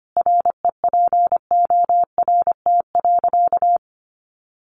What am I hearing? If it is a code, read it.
Morse code, heard as a keyed tone, about 25 wpm, 700 Hz. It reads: REPORT.
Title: Sentences from Top 1300 Words 25wpm